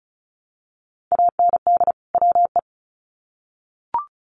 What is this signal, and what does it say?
Morse code, heard as a keyed tone, about 34 wpm, 700 Hz. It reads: AND WE
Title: Sets of 2 Words 35wpm